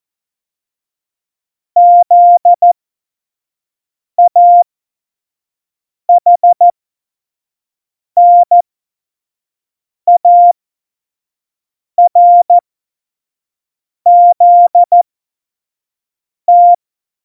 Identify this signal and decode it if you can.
Morse code, heard as a keyed tone, about 14 wpm, spaced as for 5 wpm, 700 Hz. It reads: ZAHNARZT
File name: Morsekurs / Witze / Witz 19. Dezember 2017, 25 F70